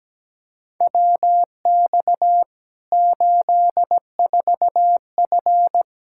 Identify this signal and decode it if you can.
Morse code, heard as a keyed tone, about 17 wpm, 700 Hz. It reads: WX 84F
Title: Practice QSO - Exchange 2 - Info - 17wpm